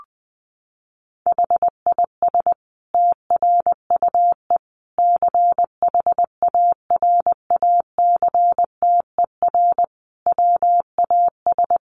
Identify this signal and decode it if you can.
Morse code, heard as a keyed tone, about 20 wpm, 700 Hz. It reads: HIS TRUE CHARACTER WAS
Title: Sentences from Top 1200 Words 20wpm